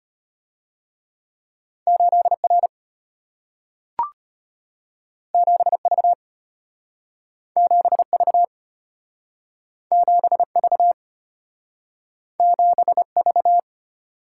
Morse code, heard as a keyed tone, about 34 wpm, 700 Hz. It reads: 8R 7V 7V 7V 7V